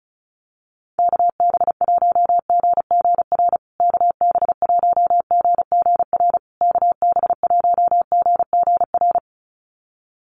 Morse code, heard as a keyed tone, about 34 wpm, 700 Hz. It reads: KB1GGR KB1GGR KB1GGR
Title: USA Call Signs Repeated 3x 35wpm